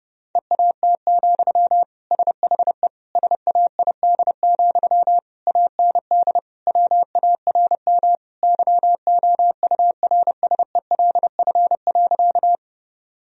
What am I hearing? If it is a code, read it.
Morse code, heard as a keyed tone, about 32 wpm, 700 Hz. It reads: EAT, SHE SAID, AND WARM YOURSELF.
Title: Sentences from Top 1000 Words - Review Part 2 - 30wpm